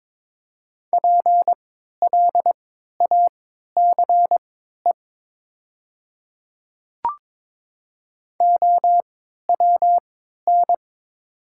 Morse code, heard as a keyed tone, about 21 wpm, spaced as for 12 wpm, 700 Hz. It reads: PLACE OWN
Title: 3x Character Spacing - Top 300 Words - 22wpm